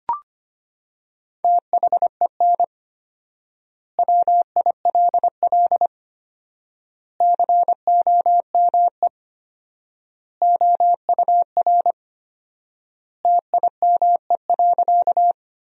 Morse code, heard as a keyed tone, about 25 wpm, 700 Hz. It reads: THEN WILL COME OUR TIME.
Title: 4x Word-Spacing - Sentences from Top 200 Words - 25wpm